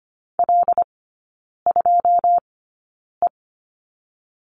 Morse code, heard as a keyed tone, about 25 wpm, spaced as for 9 wpm, 700 Hz. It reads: L2E
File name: course / 25:10wpm / 093 Lesson 16e - Letter c - Three Characters 25wpm